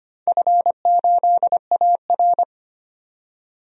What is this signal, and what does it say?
Morse code, heard as a keyed tone, about 25 wpm, 700 Hz. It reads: F8AR
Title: DXCC-100 Call Signs - 4 Characters - 25wpm